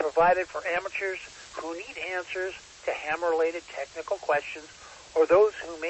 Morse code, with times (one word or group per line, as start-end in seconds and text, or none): none